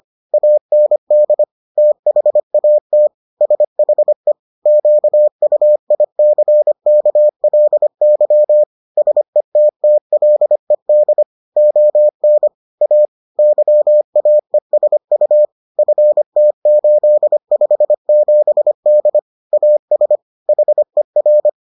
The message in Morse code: AND THAT SHE QUICKLY SETTLED ON A YAESU FT857D AS HER

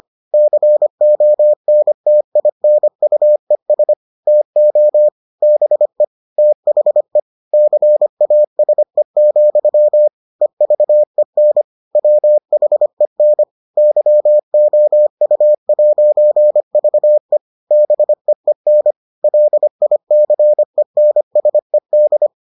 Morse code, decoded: CONTINUES TO BE THE CASE, EVEN WHEN YOU'VE BEEN LICENSED

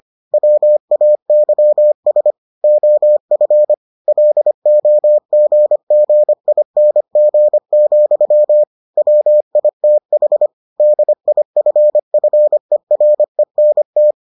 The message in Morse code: WAYS OF LOGGING, WITH DIFFERENT

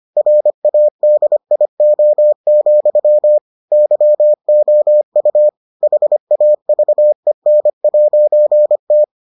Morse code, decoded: RADIO, YOU HAVEN'T

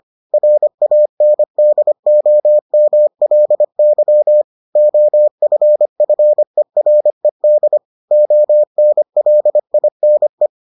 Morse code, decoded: RANDOMLY OFFERED ONLINE

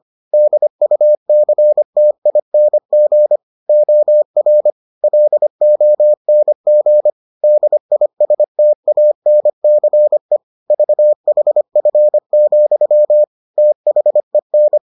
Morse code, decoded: DUCTING OR LONG DISTANCE VHF, THEN

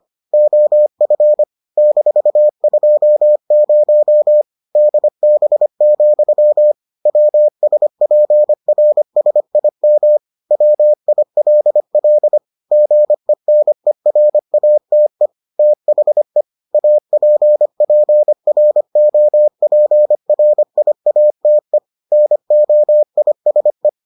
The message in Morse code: OF -20 DB, WSPRSIM WILL GENERATE THE APPROPRIATE NOISE

